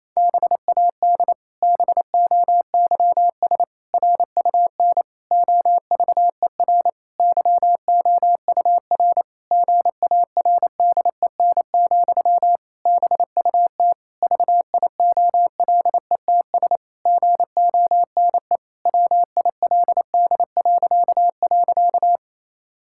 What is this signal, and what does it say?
Morse code, heard as a keyed tone, about 28 wpm, 700 Hz. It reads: BAD BOYS RUN OVER YOUR GARDEN, BUT VIOLETS GONE WILD..
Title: Pangram Challenge v2 - 28wpm